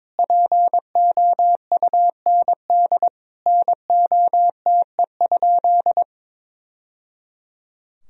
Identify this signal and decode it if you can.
Morse code, heard as a keyed tone, about 21 wpm, 700 Hz. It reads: POUND NOTE?
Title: Sentences from Top 1000 Words - Review Part 2 - 22wpm